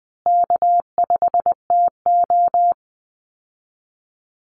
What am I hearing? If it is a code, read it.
Morse code, heard as a keyed tone, about 20 wpm, 700 Hz. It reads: K5TO
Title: Easy Call Signs Repeated 2x - 20wpm